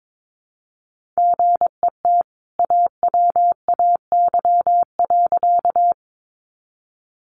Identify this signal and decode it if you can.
Morse code, heard as a keyed tone, about 22 wpm, 700 Hz. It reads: GET AWAY.
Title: Sentences from Top 600 Words 22wpm